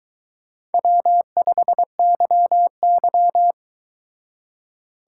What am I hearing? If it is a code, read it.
Morse code, heard as a keyed tone, about 24 wpm, 700 Hz. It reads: W5YY